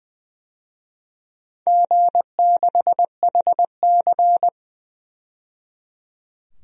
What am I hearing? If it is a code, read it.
Morse code, heard as a keyed tone, about 20 wpm, 700 Hz. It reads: G6HC
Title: DXCC-100 Call Signs - 4 Characters - 20wpm